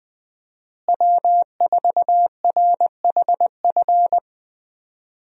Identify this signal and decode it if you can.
Morse code, heard as a keyed tone, about 21 wpm, 700 Hz. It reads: W4RHF
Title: USA Call Signs - 5 Characters - Repeated 3x - 20wpm